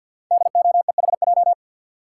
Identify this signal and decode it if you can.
Morse code, heard as a keyed tone, about 52 wpm, 690 Hz. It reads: DOEHJ
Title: Code Groups - 5 Letters - 50wpm